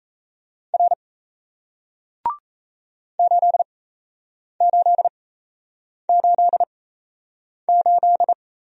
Morse code, heard as a keyed tone, about 38 wpm, spaced as for 8 wpm, 700 Hz. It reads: R 8888